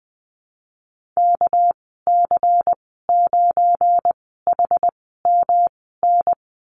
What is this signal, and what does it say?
Morse code, heard as a keyed tone, about 20 wpm, spaced as for 15 wpm, 700 Hz. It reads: KC9HMN
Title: 231 Lesson 37g - Letter x - Call Signs 20wpm